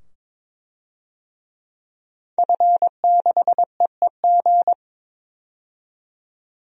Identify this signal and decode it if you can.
Morse code, heard as a keyed tone, about 23 wpm, 700 Hz. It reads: F6EEG
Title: DXCC-100 Call Signs - 5 Characters - 22wpm